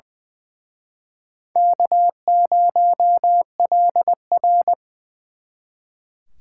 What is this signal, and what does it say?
Morse code, heard as a keyed tone, about 25 wpm, 700 Hz. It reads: K0LR
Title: Speed Racing - USA Call Signs 20wpm